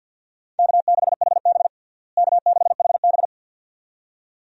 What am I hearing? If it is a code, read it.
Morse code, heard as a keyed tone, about 49 wpm, 700 Hz. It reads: K6HB K6HB
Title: USA Call Signs - 4 Characters - Repeated 2x - 50wpm